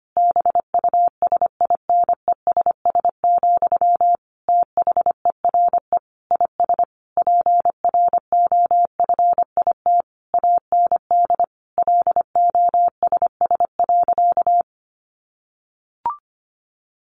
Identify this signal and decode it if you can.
Morse code, heard as a keyed tone, about 25 wpm, 700 Hz. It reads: BUSINESS, THERE IS PROFIT AND LOSS.
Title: Sentences from Top 1400 Words 25wpm